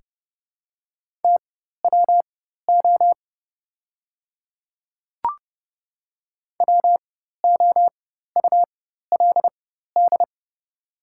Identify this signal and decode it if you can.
Morse code, heard as a keyed tone, about 30 wpm, spaced as for 14 wpm, 700 Hz. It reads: TWO WOULD
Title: 4x Character Spacing - Top 200 Words - 30wpm